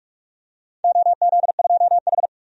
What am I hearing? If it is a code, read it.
Morse code, heard as a keyed tone, about 42 wpm, 700 Hz. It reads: OZ2H
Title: DXCC-100 Call Signs - 4 Characters - 45wpm